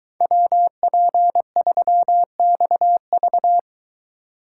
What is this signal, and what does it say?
Morse code, heard as a keyed tone, about 21 wpm, 700 Hz. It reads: WP3XV